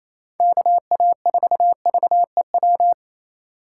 Morse code, heard as a keyed tone, about 28 wpm, 700 Hz. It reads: KA4VEW
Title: USA Call Signs - 3x Character Spacing - 28wpm